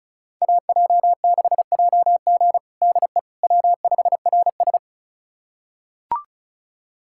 Morse code, heard as a keyed tone, about 34 wpm, 700 Hz. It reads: AJ6JG DE W5RS
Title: USA Callsign DE Callsign 35wpm